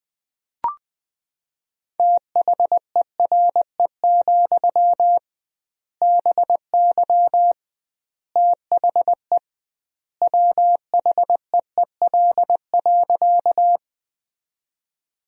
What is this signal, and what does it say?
Morse code, heard as a keyed tone, about 21 wpm, 700 Hz. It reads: THERE, BY THE WHEEL.
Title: 2x Word-Spacing - Sentences from Top 500 Words - 20wpm